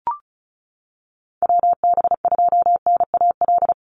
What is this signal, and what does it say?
Morse code, heard as a keyed tone, about 36 wpm, 700 Hz. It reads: WB2NAL
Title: USA Call Signs 35wpm